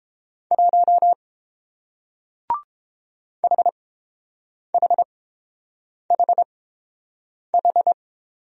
Morse code, heard as a keyed tone, about 30 wpm, spaced as for 7 wpm, 700 Hz. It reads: 1 HHHH